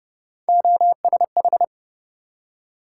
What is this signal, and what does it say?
Morse code, heard as a keyed tone, about 31 wpm, 700 Hz. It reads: OSH